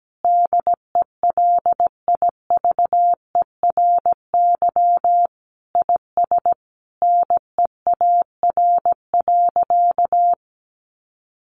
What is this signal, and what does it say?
Morse code, heard as a keyed tone, about 17 wpm, 700 Hz. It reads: DELIVERY IS NEAR.
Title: Sentences from Top 1500 Words 17wpm